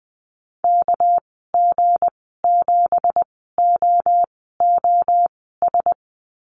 Morse code, heard as a keyed tone, about 20 wpm, spaced as for 14 wpm, 700 Hz. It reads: KG7OOS
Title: 245 Lesson 39g - Letter z - Call Signs 20wpm